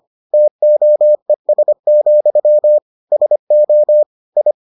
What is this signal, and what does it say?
Morse code, heard as a keyed tone, about 25 wpm, 600 Hz. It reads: TOES, SO I